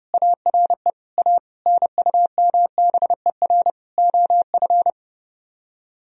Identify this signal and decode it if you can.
Morse code, heard as a keyed tone, about 30 wpm, 700 Hz. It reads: ARE A NUMBER OF